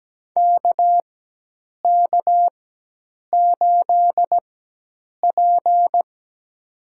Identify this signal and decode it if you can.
Morse code, heard as a keyed tone, about 17 wpm, spaced as for 8 wpm, 700 Hz. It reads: KK8P
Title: USA Call Signs - 4 Characters - 4x Character Spacing - 17wpm